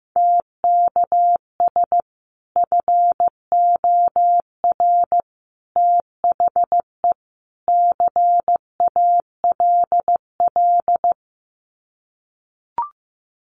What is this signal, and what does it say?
Morse code, heard as a keyed tone, about 15 wpm, 700 Hz. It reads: TKS FOR THE CALL
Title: Common QSO Phrases 15wpm